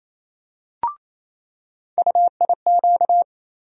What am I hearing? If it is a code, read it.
Morse code, heard as a keyed tone, about 28 wpm, 700 Hz. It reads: UIQ
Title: USA Call Sign Postfixes - 3 Letters 28wpm